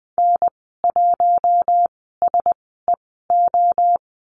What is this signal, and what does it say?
Morse code, heard as a keyed tone, about 20 wpm, spaced as for 14 wpm, 700 Hz. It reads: N1SEO
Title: 082 Lesson 14g - Number 2 - Call Signs 20wpm